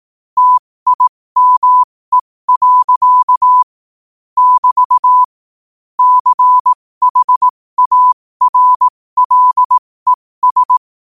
Morse code, heard as a keyed tone, about 18 wpm, spaced as for 15 wpm, 1000 Hz. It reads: TIME. <BT> CHARLES
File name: qotdfiles / 15wpm